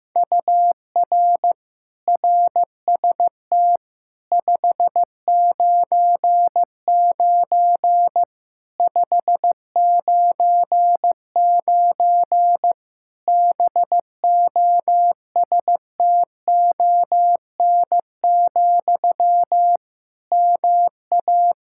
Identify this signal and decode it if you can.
Morse code, heard as a keyed tone, about 16 wpm, 700 Hz. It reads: UR RST 599 599 BOSTON, MA
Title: Practice QSO - Exchange 1 - Beginning - 15wpm